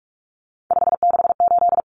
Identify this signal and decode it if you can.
Morse code, heard as a keyed tone, about 44 wpm, 700 Hz. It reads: 568
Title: Common Cut Numbers - 3-Digit 45wpm